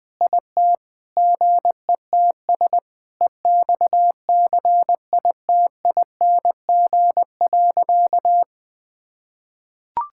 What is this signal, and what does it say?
Morse code, heard as a keyed tone, about 20 wpm, 700 Hz. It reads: IT GETS EXCITING.
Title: Sentences with 5-Word Phrases - 20wpm